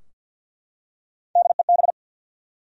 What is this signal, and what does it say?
Morse code, heard as a keyed tone, about 52 wpm, 700 Hz. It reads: DEB
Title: USA Common Names - Max 3 Letters - 50wpm